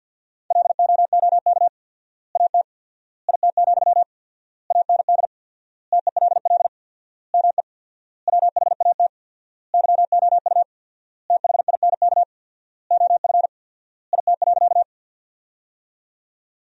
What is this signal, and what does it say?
Morse code, heard as a keyed tone, about 50 wpm, 700 Hz. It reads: LOOK AT IT, AND TELL ME WHAT YOU THINK OF IT.
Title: 4x Word-Spacing - Sentences from Top 200 Words - 50wpm